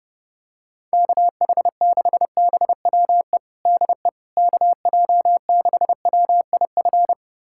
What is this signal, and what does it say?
Morse code, heard as a keyed tone, about 30 wpm, 700 Hz. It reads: KH6BWE DE KJ6WIF
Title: USA Callsign DE Callsign 30wpm